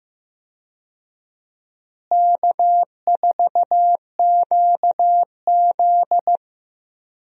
Morse code, heard as a keyed tone, about 15 wpm, 700 Hz. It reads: K4QZ
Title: USA Call Signs - 4 Characters - Repeated 2x - 15wpm